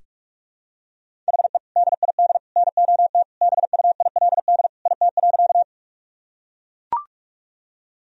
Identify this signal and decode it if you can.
Morse code, heard as a keyed tone, about 42 wpm, 700 Hz. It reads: HE DID NOT BUILD IT.
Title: Sentences from Top 1000 Words - Review Part 1 - 45wpm